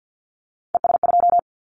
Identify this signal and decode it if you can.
Morse code, heard as a keyed tone, about 50 wpm, 690 Hz. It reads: ES2
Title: Callsign Prefixes for Top 100 DXCC - 50wpm